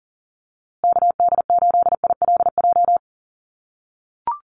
Morse code, heard as a keyed tone, about 40 wpm, 700 Hz. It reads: KD8ILJ
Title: USA Call Signs Repeated 3x 40wpm